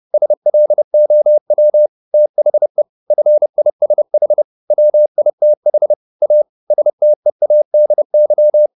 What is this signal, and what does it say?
Morse code, heard as a keyed tone, about 30 wpm, 600 Hz. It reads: SLOW THE FISH WITH A STEADY
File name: the old man and the sea / the old man and the sea 1 30wpm